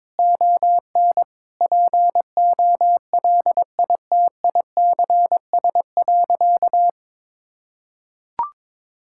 Morse code, heard as a keyed tone, about 22 wpm, 700 Hz. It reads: ON POLITICS.
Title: Sentences from Top 1000 Words - Review Part 6 - 22wpm